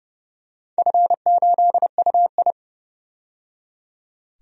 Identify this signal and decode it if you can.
Morse code, heard as a keyed tone, about 30 wpm, 700 Hz. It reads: F8UI
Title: DXCC-100 Call Signs - 4 Characters - 30wpm